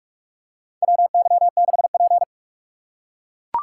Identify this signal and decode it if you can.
Morse code, heard as a keyed tone, about 42 wpm, 700 Hz. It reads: WY6P
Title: USA Call Signs - 4 Characters - 4x Character Spacing - 45wpm